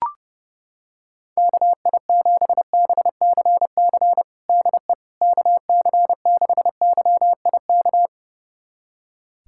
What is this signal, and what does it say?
Morse code, heard as a keyed tone, about 30 wpm, 700 Hz. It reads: KI7BCC DE KC6YIK
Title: USA Callsign DE Callsign 30wpm